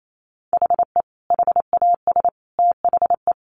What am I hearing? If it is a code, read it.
Morse code, heard as a keyed tone, about 28 wpm, 700 Hz. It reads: HE HAS THE
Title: Sentences from Top 800 Words 28wpm